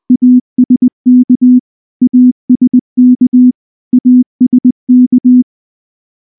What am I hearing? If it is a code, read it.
Morse code, heard as a keyed tone, about 19 wpm, 250 Hz. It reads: ASK ASK ASK
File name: morse / morse-training-20WPM-10